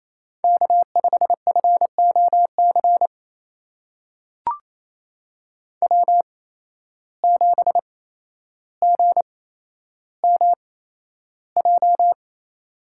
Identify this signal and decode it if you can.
Morse code, heard as a keyed tone, about 28 wpm, spaced as for 12 wpm, 700 Hz. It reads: K5FOC W7GMJ
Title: USA Call Signs - 5 Characters - 8x Character Spacing - 28wpm